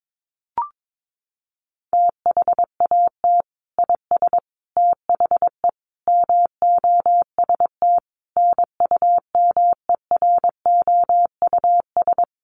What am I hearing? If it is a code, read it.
Morse code, heard as a keyed tone, about 22 wpm, 700 Hz. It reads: THAT IS THE MOST NUMEROUS
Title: Sentences from Top 1400 Words 22wpm